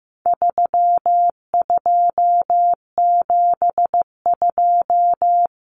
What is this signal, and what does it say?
Morse code, heard as a keyed tone, about 15 wpm, 700 Hz. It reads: 3272
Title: Common Cut Numbers - 4-Digit 15wpm